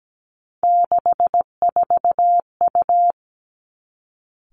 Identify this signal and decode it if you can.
Morse code, heard as a keyed tone, about 17 wpm, 700 Hz. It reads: 64U
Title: Three Challenging Letter-Number Combinations 17wpm